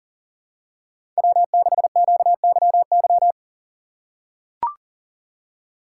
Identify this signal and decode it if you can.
Morse code, heard as a keyed tone, about 40 wpm, 700 Hz. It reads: W6QYY
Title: USA Call Signs - 4x Character Spacing - 40wpm